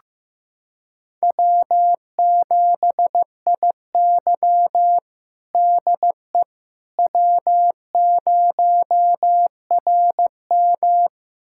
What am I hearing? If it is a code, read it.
Morse code, heard as a keyed tone, about 15 wpm, 700 Hz. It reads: W7IY DE W0RM
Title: USA Callsign DE Callsign - 4 Characters - 15wpm